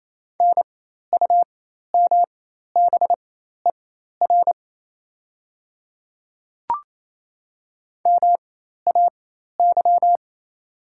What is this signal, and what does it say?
Morse code, heard as a keyed tone, about 28 wpm, spaced as for 13 wpm, 700 Hz. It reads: NUMBER MAY
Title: 4x Character Spacing - Top 100 Words - 28wpm